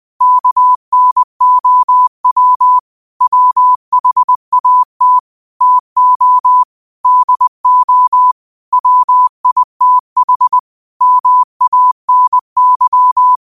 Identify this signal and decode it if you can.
Morse code, heard as a keyed tone, about 20 wpm, 1000 Hz. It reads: KNOW WHAT TO DO WITH MANY